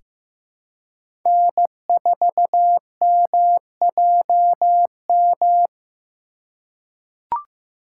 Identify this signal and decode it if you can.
Morse code, heard as a keyed tone, about 15 wpm, 700 Hz. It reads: N4MJM
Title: USA Call Signs - 5 Characters - 2x Character Spacing - 15wpm